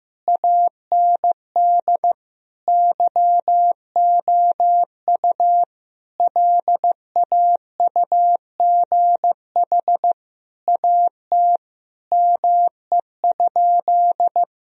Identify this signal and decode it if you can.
Morse code, heard as a keyed tone, about 15 wpm, 700 Hz. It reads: AND YOU LAUGH AT ME?
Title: Sentences from Top 1000 Words - Review Part 2 - 15wpm